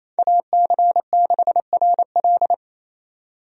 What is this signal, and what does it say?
Morse code, heard as a keyed tone, about 27 wpm, 700 Hz. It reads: AC6RL
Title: USA Call Signs - 5 Characters - 28wpm